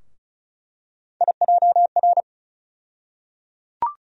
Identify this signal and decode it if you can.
Morse code, heard as a keyed tone, about 36 wpm, 700 Hz. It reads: IJR
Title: USA Call Sign Postfixes - 3 Letters 35wpm